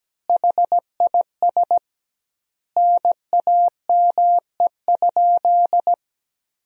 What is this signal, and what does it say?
Morse code, heard as a keyed tone, about 17 wpm, 700 Hz. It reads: HIS NAME?
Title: 2x Word-Spacing - Sentences from Top 200 Words - 17wpm